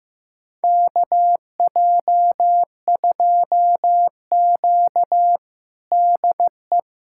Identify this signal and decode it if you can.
Morse code, heard as a keyed tone, about 16 wpm, 700 Hz. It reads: KJ2Q DE
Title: USA Callsign DE Callsign - 4 Characters - 15wpm